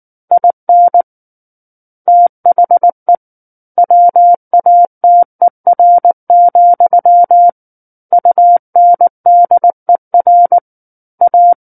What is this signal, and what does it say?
Morse code, heard as a keyed tone, about 19 wpm, 700 Hz. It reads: IN THE WATER, UNDER A
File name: Book One / 10-Chapter 10 - 19 WPM